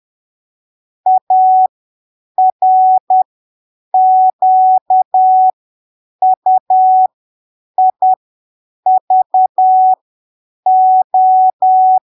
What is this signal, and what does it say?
Morse code, heard as a keyed tone, about 10 wpm, spaced as for 7 wpm, 750 Hz. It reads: ARQUIVO
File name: arquivos / texto / texto 02 10ppm